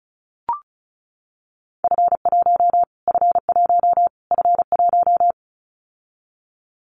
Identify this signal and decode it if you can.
Morse code, heard as a keyed tone, about 34 wpm, 700 Hz. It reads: F1 F1 F1
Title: Callsign Prefixes for Top 25 DXCC - Repeated 3x - 35wpm